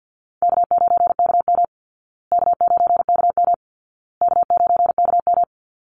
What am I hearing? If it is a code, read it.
Morse code, heard as a keyed tone, about 52 wpm, 700 Hz. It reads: K9KM K9KM K9KM
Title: Easy Call Signs Repeated 3x - 50wpm